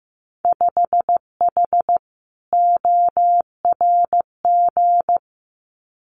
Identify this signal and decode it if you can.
Morse code, heard as a keyed tone, about 15 wpm, 700 Hz. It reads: 5H ORG
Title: Winter Field Day Exchanges Encore 15wpm